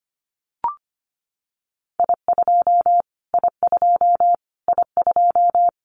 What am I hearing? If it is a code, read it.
Morse code, heard as a keyed tone, about 25 wpm, 700 Hz. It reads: I2 I2 I2
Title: Callsign Prefixes for Top 25 DXCC - Repeated 3x - 25wpm